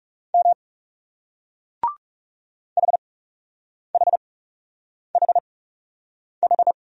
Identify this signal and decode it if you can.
Morse code, heard as a keyed tone, about 40 wpm, spaced as for 8 wpm, 700 Hz. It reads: M HHHH